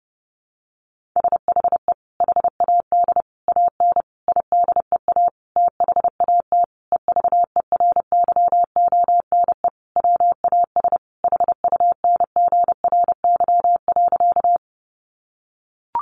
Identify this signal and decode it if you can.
Morse code, heard as a keyed tone, about 30 wpm, 700 Hz. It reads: SHE HAD AN IDEA THAT EVERYONE WAS HUNGRY.
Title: Sentences from Top 1400 Words 30wpm